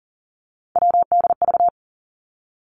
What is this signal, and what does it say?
Morse code, heard as a keyed tone, about 38 wpm, 700 Hz. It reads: WDV
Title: Three Challenging Letter-Number Combinations 40wpm